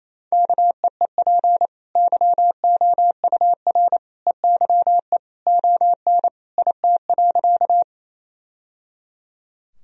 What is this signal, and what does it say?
Morse code, heard as a keyed tone, about 27 wpm, 700 Hz. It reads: KEEP YOUR EYE ON IT.
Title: Sentences from Top 1000 Words - Review Part 1 - 28wpm